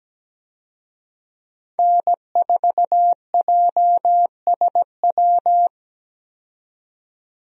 Morse code, heard as a keyed tone, about 17 wpm, 700 Hz. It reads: N4JSW
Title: USA Call Signs - 5 Characters - 17wpm